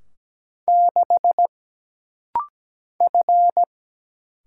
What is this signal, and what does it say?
Morse code, heard as a keyed tone, about 17 wpm, 700 Hz. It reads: 6 F
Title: Single Letter-Number - Rapid-Fire 17wpm